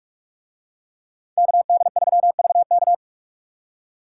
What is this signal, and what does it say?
Morse code, heard as a keyed tone, about 41 wpm, 700 Hz. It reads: KD3VK